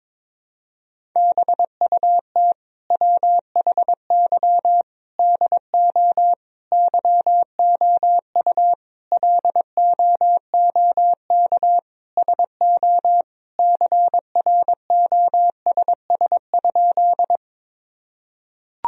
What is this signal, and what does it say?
Morse code, heard as a keyed tone, about 22 wpm, 700 Hz. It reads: BUT WHY DO YOU LOOK SO CROSS?
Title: Sentences from Top 1000 Words - Review Part 1 - 22wpm